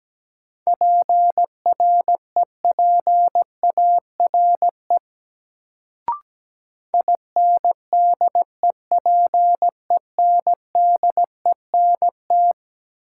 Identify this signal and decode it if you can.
Morse code, heard as a keyed tone, about 17 wpm, 700 Hz. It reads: PREPARE INDEPENDENT
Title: Sentences from Top 900 Words - New Word Drill - 17wpm